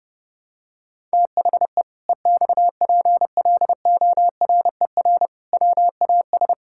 Morse code, heard as a keyed tone, about 32 wpm, 700 Hz. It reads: THE EXPLORER WAS
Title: Pangram Challenge v2 - 30wpm